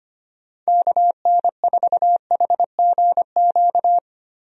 Morse code, heard as a keyed tone, about 25 wpm, 700 Hz. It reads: KN4HGQ
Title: USA Call Signs - 6 Characters - Repeated 3x - 25wpm